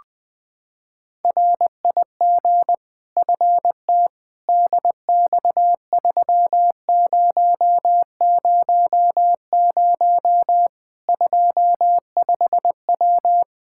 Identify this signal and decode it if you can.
Morse code, heard as a keyed tone, about 20 wpm, 700 Hz. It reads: RIG FT DX3000 25W
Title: Practice QSO - Exchange 2 - Info - 20wpm